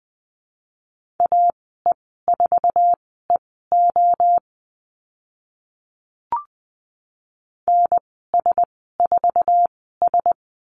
Morse code, heard as a keyed tone, about 20 wpm, spaced as for 14 wpm, 700 Hz. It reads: AE4EO NS4S
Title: 069 Lesson 12g - Letter d - Call Signs 20wpm